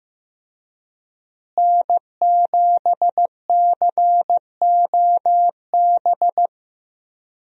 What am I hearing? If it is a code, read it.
Morse code, heard as a keyed tone, about 15 wpm, 700 Hz. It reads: N7COB
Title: USA Call Signs - 5 Characters - Repeated 3x - 15wpm